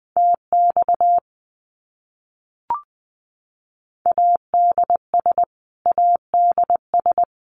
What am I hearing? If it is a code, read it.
Morse code, heard as a keyed tone, about 20 wpm, 700 Hz. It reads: TX ADS ADS
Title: Abbreviations Repeated 3x - 20wpm